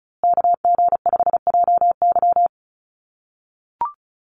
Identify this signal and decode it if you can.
Morse code, heard as a keyed tone, about 34 wpm, 700 Hz. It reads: KG5JY
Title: USA Call Signs Repeated 3x 35wpm